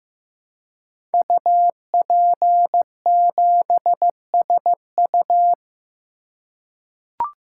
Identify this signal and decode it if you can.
Morse code, heard as a keyed tone, about 15 wpm, 700 Hz. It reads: UP7SU
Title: Code Groups - 5 Letter:Numbers - 15wpm